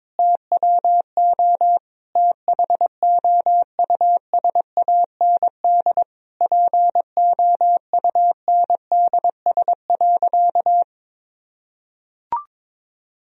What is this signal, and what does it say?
Morse code, heard as a keyed tone, about 22 wpm, 700 Hz. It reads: TWO THOUSAND POUNDS.
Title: Sentences from Top 1000 Words - Review Part 4 - 22wpm